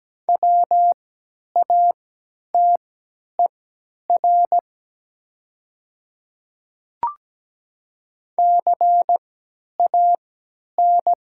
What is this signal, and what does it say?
Morse code, heard as a keyed tone, about 17 wpm, spaced as for 10 wpm, 700 Hz. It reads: WATER CAN
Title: 3x Character Spacing - Top 300 Words - 17wpm